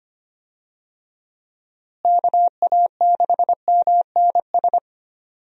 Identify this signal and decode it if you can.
Morse code, heard as a keyed tone, about 25 wpm, 700 Hz. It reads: KA6MNS
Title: USA Call Signs - 6 Characters - Repeated 3x - 25wpm